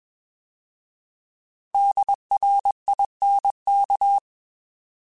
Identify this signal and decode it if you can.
Morse code, heard as a keyed tone, about 21 wpm, 800 Hz. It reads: DRINK
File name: Alphabetical / Alphabetical mp3s / Ds @20